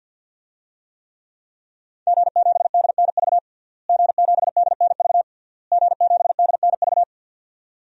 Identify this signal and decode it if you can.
Morse code, heard as a keyed tone, about 51 wpm, 700 Hz. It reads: G7DNV G7DNV G7DNV
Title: DXCC-100 Call Signs - 5 Characters - Repeated 3x - 50wpm